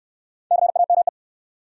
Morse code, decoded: CAME

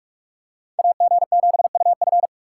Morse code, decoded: AG7UF

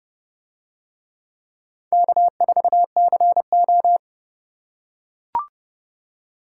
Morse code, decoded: K4CO

